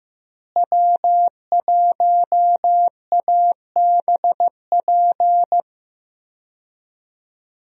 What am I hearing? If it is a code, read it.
Morse code, heard as a keyed tone, about 15 wpm, 700 Hz. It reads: W1ABP
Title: USA Call Signs - 5 Characters - Repeated 2x - 15wpm